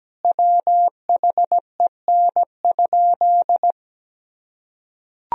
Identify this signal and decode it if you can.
Morse code, heard as a keyed tone, about 17 wpm, 700 Hz. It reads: WHEN?
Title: Sentences from Top 1000 Words - Review Part 1 - 17wpm